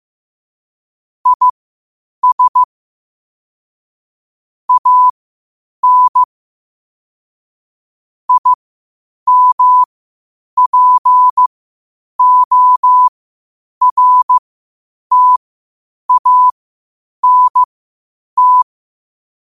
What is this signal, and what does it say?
Morse code, heard as a keyed tone, about 15 wpm, spaced as for 8 wpm, 1000 Hz. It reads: IS AN IMPORTANT
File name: qotdfiles / 8wpm